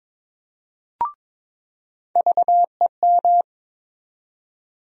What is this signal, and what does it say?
Morse code, heard as a keyed tone, about 22 wpm, 700 Hz. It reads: VEM
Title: USA Call Sign Postfixes - 3 Letters 22wpm